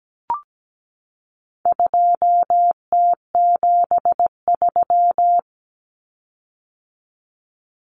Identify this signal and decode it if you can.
Morse code, heard as a keyed tone, about 17 wpm, 700 Hz. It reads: 2T73
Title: Common Cut Numbers - 4-Digit 17wpm